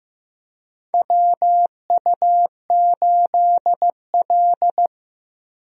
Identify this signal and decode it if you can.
Morse code, heard as a keyed tone, about 15 wpm, 700 Hz. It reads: WU8L
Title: USA Call Signs - 4 Characters - Repeated 3x - 15wpm